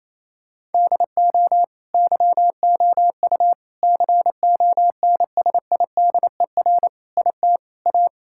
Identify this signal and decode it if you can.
Morse code, heard as a keyed tone, about 28 wpm, 700 Hz. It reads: DO YOU CONSIDER IT A
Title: Sentences from Top 1000 Words - Review Part 6 - 28wpm